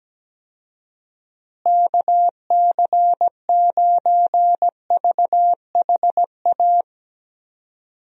Morse code, heard as a keyed tone, about 17 wpm, 700 Hz. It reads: KC9VHA
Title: USA Call Signs - 6 Characters - Repeated 3x - 17wpm